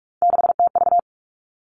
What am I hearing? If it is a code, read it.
Morse code, heard as a keyed tone, about 46 wpm, 700 Hz. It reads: 6TV
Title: Three Challenging Letter-Number Combinations 45wpm